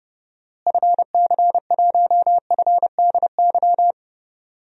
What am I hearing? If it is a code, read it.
Morse code, heard as a keyed tone, about 30 wpm, 700 Hz. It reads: FC1FDY
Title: DXCC-100 Call Signs - 6 Characters - Repeated 3x - 30wpm